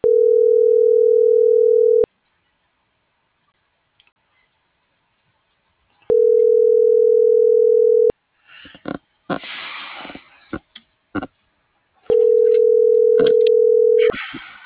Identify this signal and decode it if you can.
Ringback tone (North America), 440 + 480 Hz.